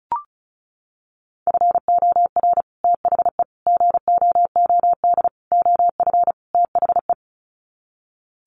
Morse code, heard as a keyed tone, about 34 wpm, 700 Hz. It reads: FOR THE GOOD OF THE
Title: Sets of 5 Words Encore2 35wpm